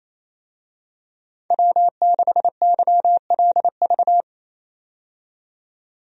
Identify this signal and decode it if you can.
Morse code, heard as a keyed tone, about 27 wpm, 700 Hz. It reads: W6YLV
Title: USA Call Signs - 5 Characters - 28wpm